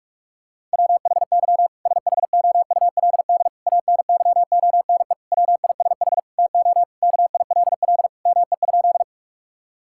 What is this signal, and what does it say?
Morse code, heard as a keyed tone, about 47 wpm, 700 Hz. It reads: WHY SHOULD ANYONE WISH TO KILL ME?
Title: Sentences from Top 1600 Words 45wpm